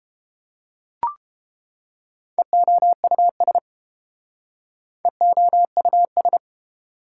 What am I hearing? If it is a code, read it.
Morse code, heard as a keyed tone, about 29 wpm, 700 Hz. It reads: EOUS EOUS